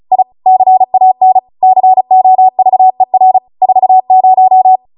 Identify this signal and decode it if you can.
Morse code, heard as a keyed tone, about 36 wpm, 750 Hz. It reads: I CAN COVER 40